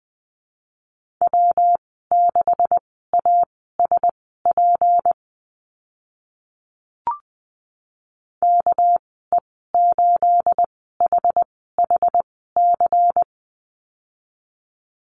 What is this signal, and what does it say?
Morse code, heard as a keyed tone, about 20 wpm, spaced as for 14 wpm, 700 Hz. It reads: W6ASP KE8HHC
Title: 238 Lesson 38g - Letter q - Call Signs 20wpm